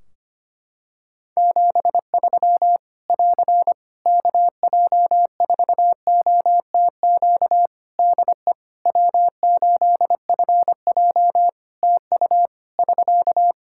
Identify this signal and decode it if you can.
Morse code, heard as a keyed tone, about 25 wpm, 700 Hz. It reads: 73 <AR> KJ4OTQ DE W8FJ TU <SK>
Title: Practice QSO - Exchange 4 - Ending - 25wpm